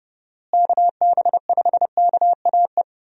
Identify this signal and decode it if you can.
Morse code, heard as a keyed tone, about 30 wpm, 700 Hz. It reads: KB5KAE
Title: USA Call Signs - 6 Characters - Repeated 2x - 30wpm